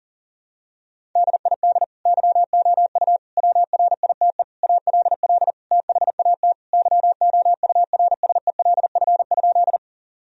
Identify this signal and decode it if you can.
Morse code, heard as a keyed tone, about 40 wpm, 700 Hz. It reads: DID YOU WRITE ALL THAT YOURSELF?
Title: Sentences from Top 1000 Words - Review Part 1 - 40wpm